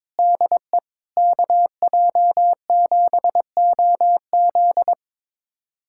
Morse code, heard as a keyed tone, about 22 wpm, 700 Hz. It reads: DE KJ7OZ
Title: USA Callsign DE Callsign 22wpm